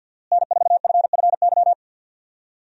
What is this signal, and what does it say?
Morse code, heard as a keyed tone, about 51 wpm, 700 Hz. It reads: N4FFY
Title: USA Call Signs - 5 Characters - Repeated 3x - 50wpm